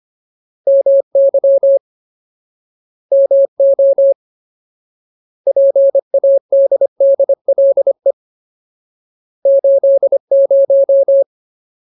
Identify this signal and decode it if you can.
Morse code, heard as a keyed tone, about 25 wpm, 550 Hz. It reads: MY MO PADDLE 80
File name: Ultimate Word List / 25WPM Ultimate